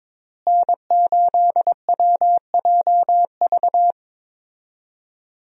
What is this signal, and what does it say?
Morse code, heard as a keyed tone, about 22 wpm, 700 Hz. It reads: N8WJV
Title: USA Call Signs - 5 Characters - Repeated 3x - 22wpm